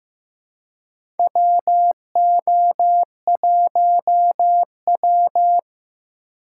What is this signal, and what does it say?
Morse code, heard as a keyed tone, about 15 wpm, 700 Hz. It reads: WO1W
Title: USA Call Signs - 4 Characters - 4x Character Spacing - 15wpm